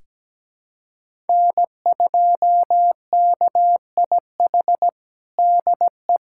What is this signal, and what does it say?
Morse code, heard as a keyed tone, about 17 wpm, 700 Hz. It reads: N2KIH DE
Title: USA Callsign DE Callsign - 5 Characters - 17wpm